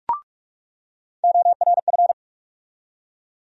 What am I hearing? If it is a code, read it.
Morse code, heard as a keyed tone, about 46 wpm, 700 Hz. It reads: ORF
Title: USA Call Sign Postfixes - 3 Letters 45wpm